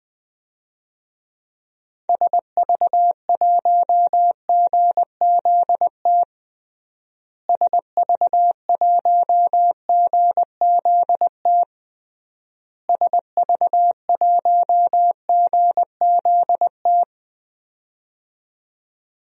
Morse code, heard as a keyed tone, about 20 wpm, 700 Hz. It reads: SV1GZT SV1GZT SV1GZT
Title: DXCC-100 Call Signs - 6 Characters - Repeated 3x - 20wpm